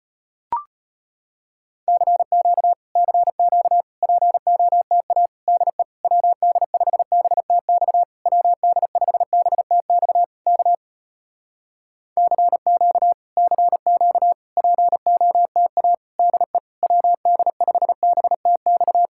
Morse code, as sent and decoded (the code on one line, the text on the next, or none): -.-. --.- / -.-. --.- / .--. --- - .- / -.. / .-- -.. ..... -... - -..- / .-- -.. ..... -... - -..- / -.- / -.-. --.- / -.-. --.- / .--. --- - .- / -.. / .-- -.. ..... -... - -..-
CQ CQ POTA DE WD5BTX WD5BTX K CQ CQ POTA DE WD5BTX